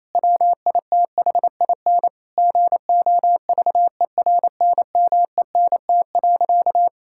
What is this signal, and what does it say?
Morse code, heard as a keyed tone, about 28 wpm, 700 Hz. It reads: WITHIN GOVERNMENT.
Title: Sentences from Top 1000 Words - Review Part 5 - 28wpm